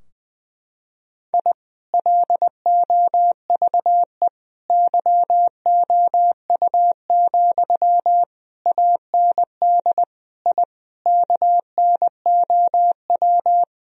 Morse code, heard as a keyed tone, about 20 wpm, 700 Hz. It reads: I LOVE YOU, AND I KNOW
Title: Sentences from Top 1000 Words - Review Part 2 - 20wpm